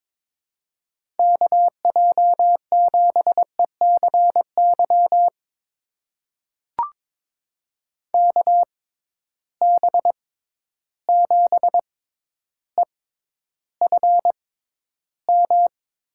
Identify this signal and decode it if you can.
Morse code, heard as a keyed tone, about 22 wpm, spaced as for 11 wpm, 700 Hz. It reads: KJ7ECY KB7EFM
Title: USA Call Signs - 6 Characters - 6x Character Spacing - 22wpm